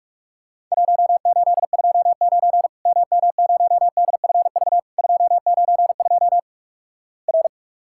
Morse code, heard as a keyed tone, about 44 wpm, 690 Hz. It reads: 1829 MM0DFV 292 R
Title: Worked All Europe DX Contest - QTC Exchange - 45wpm